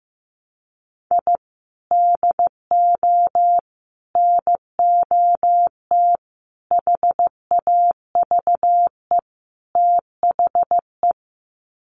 Morse code, heard as a keyed tone, about 15 wpm, 700 Hz. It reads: I DO NOT HAVE THE
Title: Sets of 5 Words 15wpm